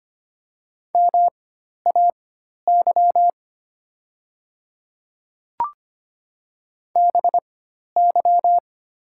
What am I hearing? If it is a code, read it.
Morse code, heard as a keyed tone, about 25 wpm, spaced as for 12 wpm, 700 Hz. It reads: MAY BY
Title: 4x Character Spacing - Top 200 Words - 25wpm